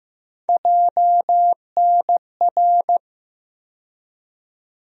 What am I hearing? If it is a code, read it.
Morse code, heard as a keyed tone, about 15 wpm, 700 Hz. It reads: JNR
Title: USA Call Sign Postfixes - 3 Letters 15wpm